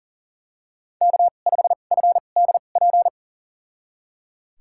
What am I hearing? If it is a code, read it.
Morse code, heard as a keyed tone, about 38 wpm, spaced as for 28 wpm, 700 Hz. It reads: K5FDP
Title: USA Call Signs - 5 Characters - 2x Character Spacing - 40wpm